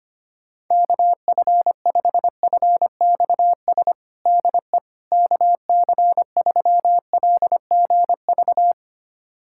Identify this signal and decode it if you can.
Morse code, heard as a keyed tone, about 25 wpm, 700 Hz. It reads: KF5FXS DE KC3LGV
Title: USA Callsign DE Callsign - 6 Characters - 25wpm